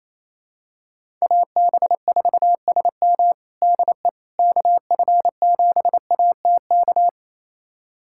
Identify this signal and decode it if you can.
Morse code, heard as a keyed tone, about 28 wpm, 700 Hz. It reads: AB4SM DE KF7ATK
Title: USA Callsign DE Callsign 28wpm